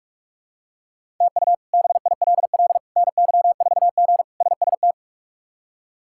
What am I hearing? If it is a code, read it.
Morse code, heard as a keyed tone, about 42 wpm, 700 Hz. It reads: TU BILL NY4G SST
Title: K1USN SST QSO Practice - 45wpm